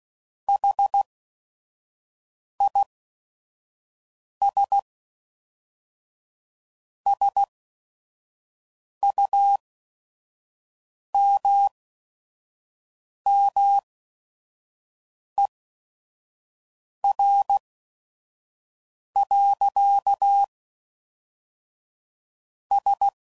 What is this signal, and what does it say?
Morse code, heard as a keyed tone, about 16 wpm, spaced as for 5 wpm, 800 Hz. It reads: HIS SUMMER. S
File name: CW Code Groups @5WPM / QSOs / QSO WD0FBT